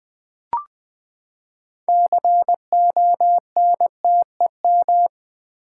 Morse code, heard as a keyed tone, about 20 wpm, 700 Hz. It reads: CONTEM
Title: Build A Word - 6 Letters At A Time - 12-Letter Words - 20wpm